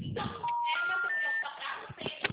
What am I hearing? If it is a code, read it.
Special information tone, with noise.